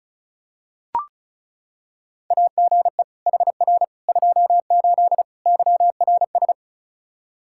Morse code, heard as a keyed tone, about 34 wpm, 700 Hz. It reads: AGE HR 28 YRS
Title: Practice QSO - Exchange 3 - More Info - 35wpm